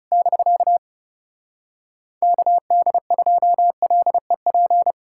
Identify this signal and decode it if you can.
Morse code, heard as a keyed tone, about 40 wpm, 700 Hz. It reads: <BK> KD2LEP